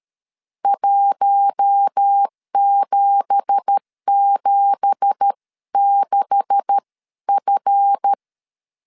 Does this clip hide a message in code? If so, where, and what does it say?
Yes: Morse code from 0.7 to 8.1 s, heard as a keyed tone, about 13 wpm, spaced as for 11 wpm, 800 Hz. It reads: 1776F